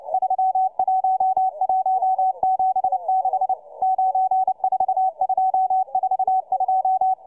Morse code, heard as a keyed tone, about 29 wpm, 760 Hz, with radio noise. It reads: 31176 94242